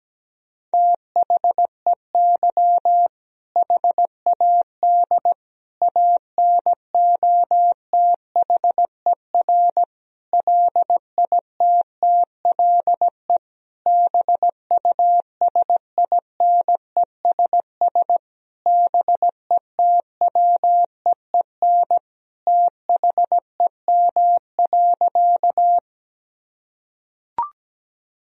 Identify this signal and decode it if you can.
Morse code, heard as a keyed tone, about 17 wpm, 700 Hz. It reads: THEY HAD ANOTHER LITTLE BUSINESS BETWEEN THEM.
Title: Sentences from Top 1000 Words - Review Part 1 - 17wpm